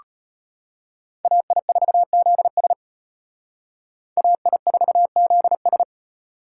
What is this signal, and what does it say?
Morse code, heard as a keyed tone, about 29 wpm, 700 Hz. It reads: AI4ZS AI4ZS
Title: Speed Racing - USA 2x2 Call Signs 25wpm